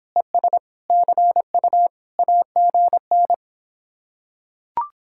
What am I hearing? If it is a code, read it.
Morse code, heard as a keyed tone, about 26 wpm, 700 Hz. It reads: ES CU AGN